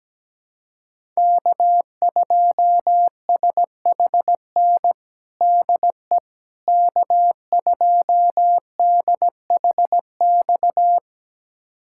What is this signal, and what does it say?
Morse code, heard as a keyed tone, about 17 wpm, 700 Hz. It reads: K2SHN DE K2DHX
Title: USA Callsign DE Callsign - 5 Characters - 17wpm